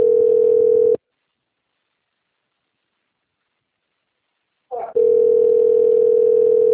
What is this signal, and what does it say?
Ringback tone (North America), 440 + 480 Hz.
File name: out- - - - - .43694